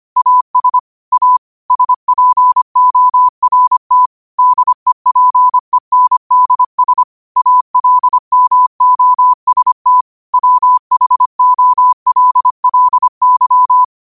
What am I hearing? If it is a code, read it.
Morse code, heard as a keyed tone, about 25 wpm, 1000 Hz. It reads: AS A SPORT DEPENDS ALMOST WHOLLY